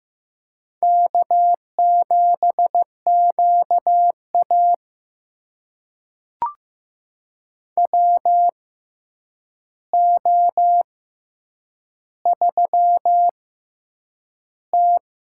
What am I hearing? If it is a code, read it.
Morse code, heard as a keyed tone, about 15 wpm, spaced as for 8 wpm, 700 Hz. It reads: K7QA WO3T
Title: USA Call Signs - 4 Characters - 6x Character Spacing - 15wpm